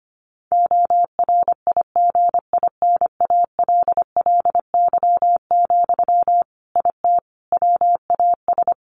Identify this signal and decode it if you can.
Morse code, heard as a keyed tone, about 25 wpm, 700 Hz. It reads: ORIGINALLY, IT WAS
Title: Sentences from Top 1500 Words 25wpm